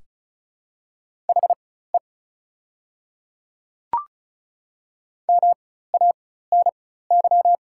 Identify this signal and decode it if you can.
Morse code, heard as a keyed tone, about 34 wpm, spaced as for 16 wpm, 700 Hz. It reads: HE MANY
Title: 4x Character Spacing - Top 100 Words - 35wpm